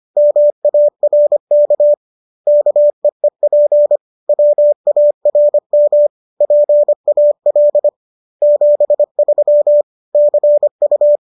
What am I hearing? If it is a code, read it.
Morse code, heard as a keyed tone, about 25 wpm, 590 Hz. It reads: MARK KEEP WARM PAL 73 CU
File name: qso213 25